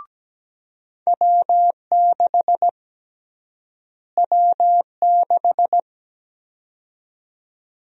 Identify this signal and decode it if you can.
Morse code, heard as a keyed tone, about 17 wpm, 700 Hz. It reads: W6 W6
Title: USA Call Sign Prefixes - Repeated 2x - 17wpm